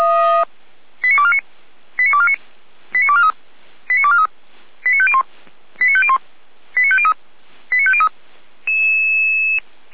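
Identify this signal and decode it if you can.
Five-tone selective call (ZVEI), heard as quick runs of five tones, tones of about 70 ms. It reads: 89249 89249 89243 89243 89501 89501 89503 89503
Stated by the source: ZVEI five-tone selective call received on SDR.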